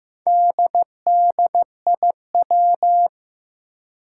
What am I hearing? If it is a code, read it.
Morse code, heard as a keyed tone, about 15 wpm, 700 Hz. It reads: DDIW
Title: Four Challenging Letter-Number Combinations 15wpm